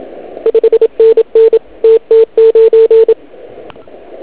Morse code, heard as a keyed tone, about 27 wpm, 450 Hz. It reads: 5NN TT9